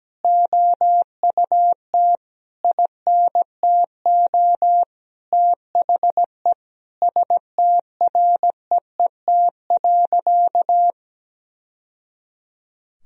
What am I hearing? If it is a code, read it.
Morse code, heard as a keyed tone, about 17 wpm, 700 Hz. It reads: OUT INTO THE STREET.
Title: Sentences from Top 1000 Words - Review Part 3 - 17wpm